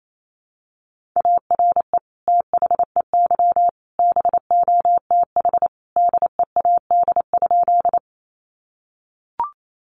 Morse code, heard as a keyed tone, about 28 wpm, 700 Hz. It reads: ARE THEY BOTH DEAD?
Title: Sentences from Top 1500 Words 28wpm